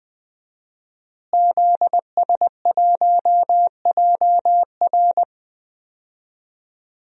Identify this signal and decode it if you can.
Morse code, heard as a keyed tone, about 20 wpm, 700 Hz. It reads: ZS1JR
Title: DXCC-100 Call Signs - 5 Characters - Repeated 3x - 20wpm